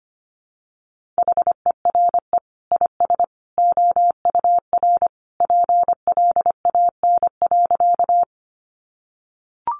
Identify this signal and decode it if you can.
Morse code, heard as a keyed tone, about 25 wpm, 700 Hz. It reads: HERE IS OUR PLAN.
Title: Sentences from Top 500 Words 25wpm